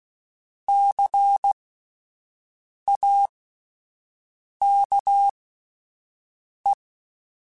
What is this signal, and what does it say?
Morse code, heard as a keyed tone, about 16 wpm, spaced as for 5 wpm, 800 Hz. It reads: CAKE
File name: Alphabetical / Alphabetical mp3s / Cs @5